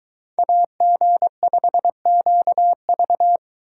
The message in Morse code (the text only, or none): AG5QV